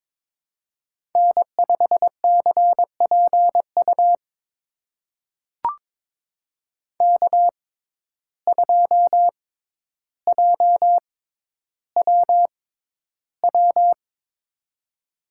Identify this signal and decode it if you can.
Morse code, heard as a keyed tone, about 22 wpm, spaced as for 11 wpm, 700 Hz. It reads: N5CPU K2JWW
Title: USA Call Signs - 5 Characters - 6x Character Spacing - 22wpm